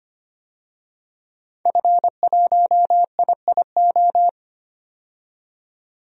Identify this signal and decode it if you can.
Morse code, heard as a keyed tone, about 25 wpm, 700 Hz. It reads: F1IIO
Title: DXCC-100 Call Signs - 5 Characters - 25wpm